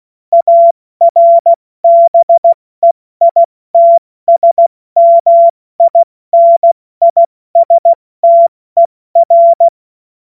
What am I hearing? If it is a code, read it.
Morse code, heard as a keyed tone, about 16 wpm, spaced as for 14 wpm, 680 Hz. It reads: ARBEITSMINISTER
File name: Witz-30-September-2025-70-F80